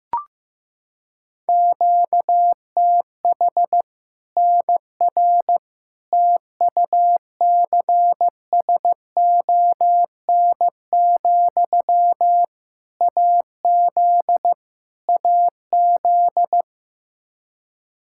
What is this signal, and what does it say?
Morse code, heard as a keyed tone, about 15 wpm, 700 Hz. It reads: QTH NR TUCSON, AZ AZ
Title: Practice QSO - Exchange 1 - Beginning - 15wpm